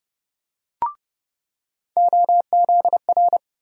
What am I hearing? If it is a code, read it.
Morse code, heard as a keyed tone, about 30 wpm, 700 Hz. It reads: OZR
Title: USA Call Sign Postfixes - 3 Letters 30wpm